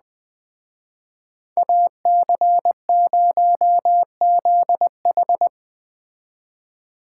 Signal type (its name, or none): Morse code